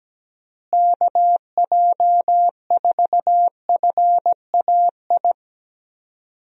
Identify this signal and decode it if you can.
Morse code, heard as a keyed tone, about 17 wpm, 700 Hz. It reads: KJ4FAI
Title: USA Call Signs - 6 Characters - Repeated 3x - 17wpm